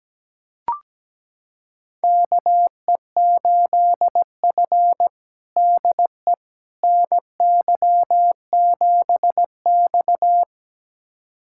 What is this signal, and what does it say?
Morse code, heard as a keyed tone, about 17 wpm, 700 Hz. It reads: KE8F DE NY7X
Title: USA Callsign DE Callsign - 4 Characters - 17wpm